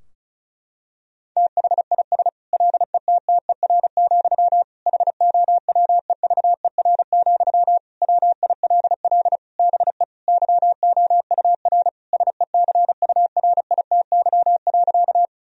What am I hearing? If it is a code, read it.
Morse code, heard as a keyed tone, about 34 wpm, 700 Hz. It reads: THIS LETTER, HOWEVER, WILL BE YOUR SECURITY.
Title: Sentences from Top 1000 Words - Review Part 4 - 35wpm